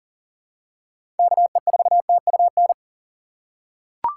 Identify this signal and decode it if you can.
Morse code, heard as a keyed tone, about 40 wpm, 700 Hz. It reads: KE4TUN
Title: USA Call Signs - 6 Characters - 2x Character Spacing - 40wpm